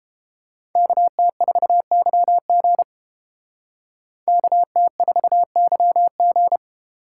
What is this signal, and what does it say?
Morse code, heard as a keyed tone, about 29 wpm, 700 Hz. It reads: KT4YG KT4YG